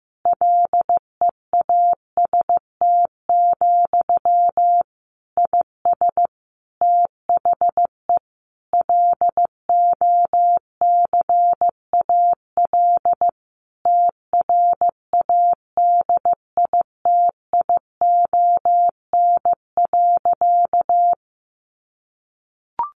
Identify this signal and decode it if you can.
Morse code, heard as a keyed tone, about 15 wpm, 700 Hz. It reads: LEAST, IS THE LOCAL TRADITION.
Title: Sentences from Top 1500 Words 15wpm